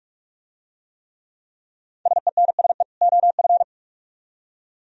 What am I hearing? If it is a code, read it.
Morse code, heard as a keyed tone, about 40 wpm, 700 Hz. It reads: SENSE OF